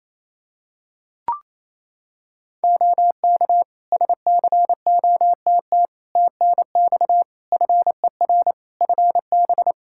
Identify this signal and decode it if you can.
Morse code, heard as a keyed tone, about 28 wpm, 700 Hz. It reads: OK SCOTT TNX FER FB
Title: Practice QSO - Exchange 4 - Ending - 28wpm